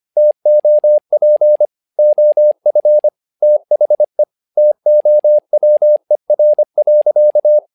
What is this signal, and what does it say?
Morse code, heard as a keyed tone, about 25 wpm, 600 Hz. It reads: TOP OF THE TOWER.
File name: learning center / tower trouble / tower trouble 25wpm